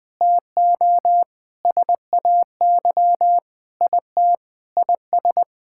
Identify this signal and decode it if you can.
Morse code, heard as a keyed tone, about 21 wpm, 700 Hz. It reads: TO SAY IT IS
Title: Sentences with 8-Word Phrases - Spoken Beginning - 20wpm